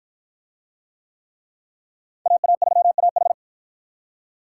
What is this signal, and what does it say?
Morse code, heard as a keyed tone, about 53 wpm, 700 Hz. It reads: AA3AH